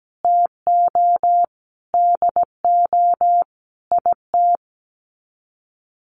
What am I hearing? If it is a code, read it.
Morse code, heard as a keyed tone, about 17 wpm, 700 Hz. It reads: TO DO IT
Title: Sets of 5 Words Encore2 17wpm